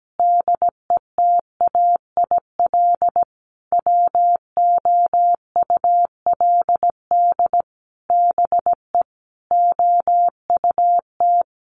Morse code, heard as a keyed tone, about 17 wpm, 700 Hz. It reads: DETAIL WOULD BE OUT
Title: Sentences from Top 1500 Words 17wpm